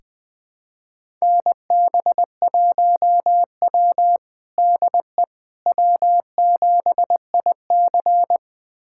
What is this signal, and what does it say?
Morse code, heard as a keyed tone, about 20 wpm, 700 Hz. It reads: NB1W DE W7IC
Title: USA Callsign DE Callsign - 4 Characters - 20wpm